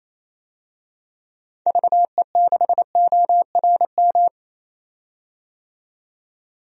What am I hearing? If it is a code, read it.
Morse code, heard as a keyed tone, about 28 wpm, 700 Hz. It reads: VE6ORM
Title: DXCC-100 Call Signs - 6 Characters - 28wpm